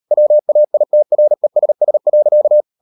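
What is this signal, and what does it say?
Morse code, heard as a keyed tone, about 38 wpm, 600 Hz. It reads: WAITRESS.
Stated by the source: Morse code practice at 38 WPM, sending Ken Follett's novel Never in English, from CW med Gnister.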